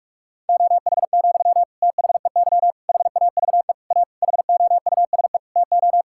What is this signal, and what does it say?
Morse code, heard as a keyed tone, about 41 wpm, 700 Hz. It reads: OH, THEY HAVE A HOUSE TO
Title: Sentences from Top 1000 Words - Review Part 5 - 45wpm